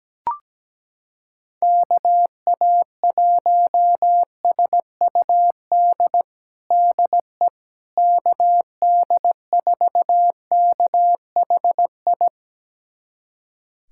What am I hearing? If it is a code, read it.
Morse code, heard as a keyed tone, about 17 wpm, 700 Hz. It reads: KA1SUD DE KD4KHI
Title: USA Callsign DE Callsign - 6 Characters - 17wpm